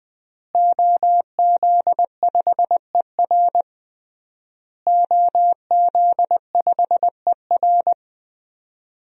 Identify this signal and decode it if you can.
Morse code, heard as a keyed tone, about 21 wpm, 700 Hz. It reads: OZ5ER OZ5ER
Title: DXCC-100 Call Signs - 5 Characters - Repeated 3x - 20wpm